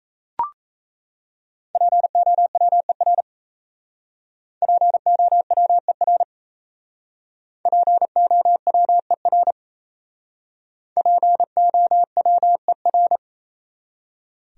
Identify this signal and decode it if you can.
Morse code, heard as a keyed tone, about 36 wpm, 700 Hz. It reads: POWER POWER POWER POWER